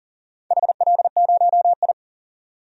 Morse code, heard as a keyed tone, about 40 wpm, 690 Hz. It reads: HL0I
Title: DXCC-100 Call Signs - 4 Characters - 40wpm